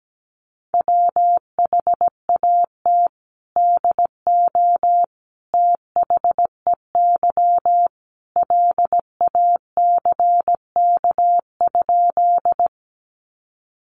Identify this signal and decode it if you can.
Morse code, heard as a keyed tone, about 17 wpm, 700 Hz. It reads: WHAT DO THEY LACK?
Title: Sentences from Top 1100 Words 17wpm